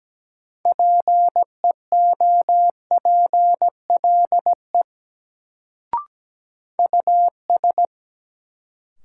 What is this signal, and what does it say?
Morse code, heard as a keyed tone, about 17 wpm, 700 Hz. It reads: PEOPLE US
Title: Sets of 2 Words - New Word Drill - 17wpm